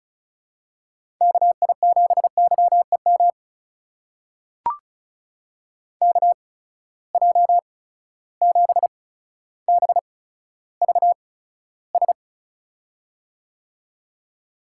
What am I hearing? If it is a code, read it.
Morse code, heard as a keyed tone, about 36 wpm, spaced as for 15 wpm, 700 Hz. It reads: KI7YEM KJ7BVS
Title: USA Call Signs - 6 Characters - 8x Character Spacing - 35wpm